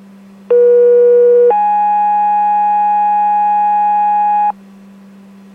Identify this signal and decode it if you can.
Two-tone page, heard as a one-second tone, then a three-second tone, about 480 then 830 Hz, with noise. It is heard from 0.5 to 4.5 s.